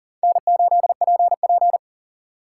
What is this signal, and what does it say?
Morse code, heard as a keyed tone, about 38 wpm, 700 Hz. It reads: N8PP
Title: USA Call Signs - 4 Characters - 6x Character Spacing - 40wpm